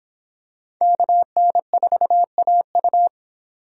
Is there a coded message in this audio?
Yes: Morse code from 0.8 to 3.1 s, heard as a keyed tone, about 26 wpm, 700 Hz. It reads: KN4AU